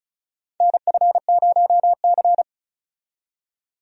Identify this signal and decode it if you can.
Morse code, heard as a keyed tone, about 36 wpm, 700 Hz. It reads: NF0C
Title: USA Call Signs - 4 Characters - 35wpm